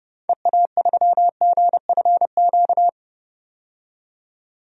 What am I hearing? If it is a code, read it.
Morse code, heard as a keyed tone, about 30 wpm, 700 Hz. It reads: EA3GFQ
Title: DXCC-100 Call Signs - 6 Characters - 30wpm